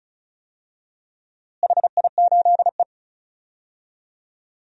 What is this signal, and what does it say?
Morse code, heard as a keyed tone, about 36 wpm, 700 Hz. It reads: HI8E
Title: DXCC-100 Call Signs - 4 Characters - Repeated 3x - 35wpm